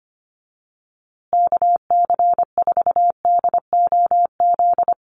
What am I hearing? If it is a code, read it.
Morse code, heard as a keyed tone, about 25 wpm, 700 Hz. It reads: KC4DOZ
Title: USA Call Signs Repeated 3x 25wpm